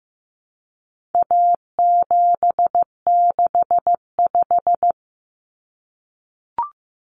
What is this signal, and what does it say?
Morse code, heard as a keyed tone, about 15 wpm, 700 Hz. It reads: A765
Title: Common Cut Numbers - 4-Digit 15wpm